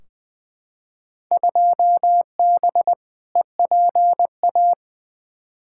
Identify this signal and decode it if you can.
Morse code, heard as a keyed tone, about 20 wpm, 700 Hz. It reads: 2B EPA
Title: Field Day Exchanges Encore - 20wpm